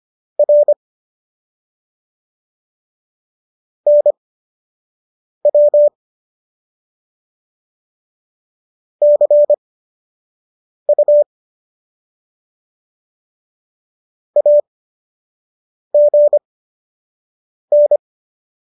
Morse code, heard as a keyed tone, about 25 wpm, spaced as for 6 wpm, 590 Hz. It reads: R NW CU AGN